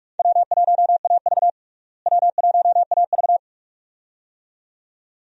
Morse code, heard as a keyed tone, about 46 wpm, 700 Hz. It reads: W1AV W1AV
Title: USA Call Signs - 4 Characters - Repeated 2x - 45wpm